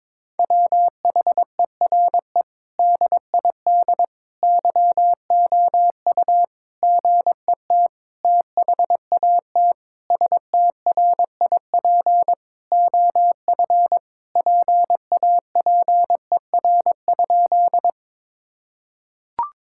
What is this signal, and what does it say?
Morse code, heard as a keyed tone, about 22 wpm, 700 Hz. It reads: WHERE DID YOU GET THAT STRIP OF PAPER?
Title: Sentences from Top 1600 Words 22wpm